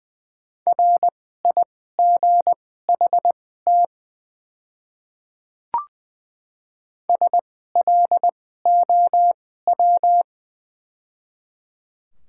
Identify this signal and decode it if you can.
Morse code, heard as a keyed tone, about 20 wpm, spaced as for 14 wpm, 700 Hz. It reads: RIGHT SLOW
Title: 2x Character Spacing - Top 500 Words - 20wpm